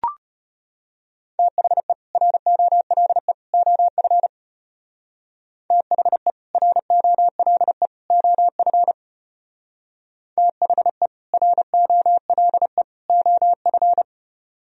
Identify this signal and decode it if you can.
Morse code, heard as a keyed tone, about 32 wpm, 700 Hz. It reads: THE ROLE OF THE ROLE OF THE ROLE OF